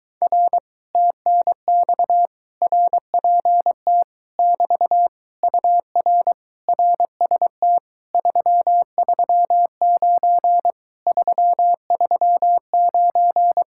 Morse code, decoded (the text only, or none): R TNX RPT <BT> UR RST 339 339